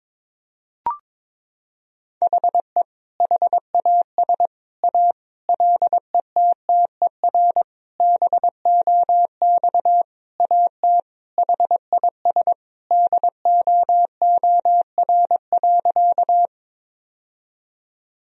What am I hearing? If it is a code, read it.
Morse code, heard as a keyed tone, about 21 wpm, 700 Hz. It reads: HE HAS A LETTER BOX AT HIS DOOR.
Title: Sentences from Top 1000 Words - Review Part 2 - 22wpm